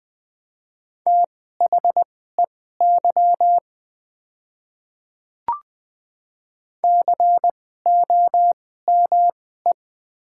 Morse code, heard as a keyed tone, about 20 wpm, spaced as for 15 wpm, 700 Hz. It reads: THEY COME
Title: 2x Character Spacing - Top 100 Words - 20wpm